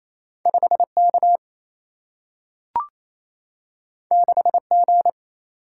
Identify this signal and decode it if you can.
Morse code, heard as a keyed tone, about 28 wpm, 700 Hz. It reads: 5 K 6 G
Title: Single-Letter Single-Number 28wpm